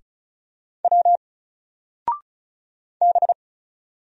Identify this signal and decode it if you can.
Morse code, heard as a keyed tone, about 35 wpm, 700 Hz. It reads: W B
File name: Single-Letter-Number-35wpm